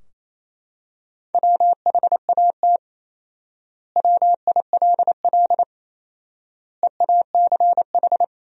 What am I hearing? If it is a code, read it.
Morse code, heard as a keyed tone, about 28 wpm, 700 Hz. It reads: WHAT WILL EACH
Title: 4x Word-Spacing - Sentences from Top 200 Words - 28wpm